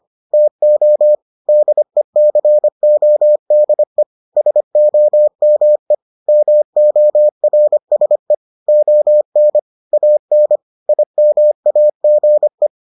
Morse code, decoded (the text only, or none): TO DECODE SOME MORSE ON AN IMAGE